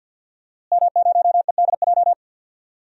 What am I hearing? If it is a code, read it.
Morse code, heard as a keyed tone, about 50 wpm, 700 Hz. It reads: M0EDJ
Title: DXCC-100 Call Signs - 5 Characters - Repeated 3x - 50wpm